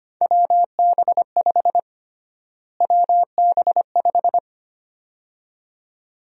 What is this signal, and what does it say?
Morse code, heard as a keyed tone, about 25 wpm, 700 Hz. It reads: WB5 WB5
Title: USA Call Sign Prefixes - Repeated 2x - 25wpm